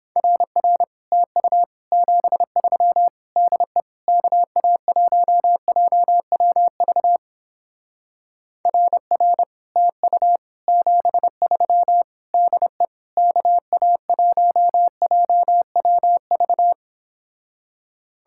Morse code, heard as a keyed tone, about 29 wpm, 700 Hz. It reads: RR TU 73 DE KA1JWV RR TU 73 DE KA1JWV